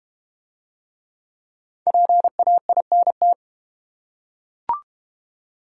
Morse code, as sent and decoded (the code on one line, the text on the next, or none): .--. .- .. -. -
PAINT